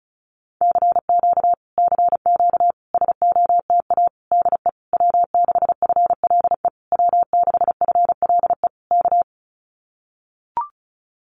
Morse code, heard as a keyed tone, about 34 wpm, 700 Hz. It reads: CQ CQ SOTA DE W6FLE W6FLE K
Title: Common QSO Phrases Encore 35wpm